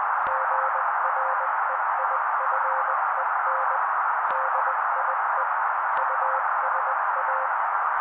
Morse code, heard as a keyed tone, about 20 wpm, 530 Hz. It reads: GREIFEN DIE USA
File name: daily / 18wpm noise